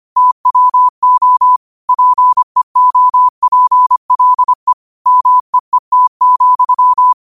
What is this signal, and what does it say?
Morse code, heard as a keyed tone, about 25 wpm, 1000 Hz. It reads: TWO PEOPLE MEET,
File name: qotdfiles / 25wpm